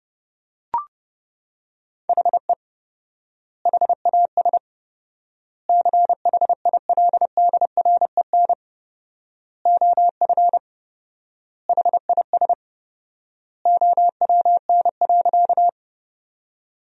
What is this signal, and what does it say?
Morse code, heard as a keyed tone, about 30 wpm, 700 Hz. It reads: HE HAS CHILDREN OF HIS OWN.
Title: 4x Word-Spacing - Sentences from Top 300 Words - 30wpm